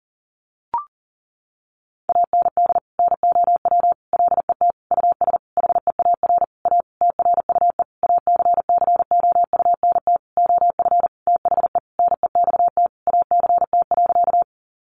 Morse code, heard as a keyed tone, about 40 wpm, 700 Hz. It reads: AND NOW LET US HEAR A TRUE ACCOUNT OF THE NEXT ACT.
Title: Sentences from Top 900 Words 40wpm